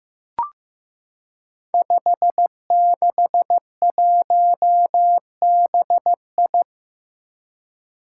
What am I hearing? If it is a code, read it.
Morse code, heard as a keyed tone, about 15 wpm, 700 Hz. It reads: 561BI
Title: Five Challenging Letter-Number Combinations 15wpm